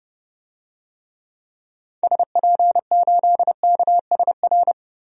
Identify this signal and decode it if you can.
Morse code, heard as a keyed tone, about 29 wpm, 700 Hz. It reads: SP8KSR
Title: DXCC-100 Call Signs - 6 Characters - Repeated 3x - 30wpm